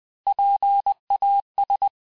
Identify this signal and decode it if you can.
Morse code, heard as a keyed tone, about 20 wpm, 800 Hz. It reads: PAS
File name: cours-du-mercredi / audios / cours-858-Fin-de-QSO-vitesse-19mpm